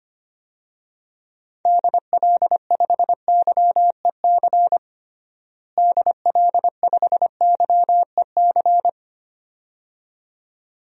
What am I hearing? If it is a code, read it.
Morse code, heard as a keyed tone, about 25 wpm, 700 Hz. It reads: DL5YEC DL5YEC
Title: DXCC-100 Call Signs - 6 Characters - Repeated 2x - 25wpm